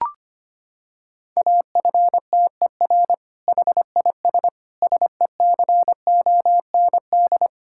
Morse code, heard as a keyed tone, about 25 wpm, 700 Hz. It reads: AFTER HIS SECOND
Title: Sentences from Top 1000 Words - Review Part 6 - 25wpm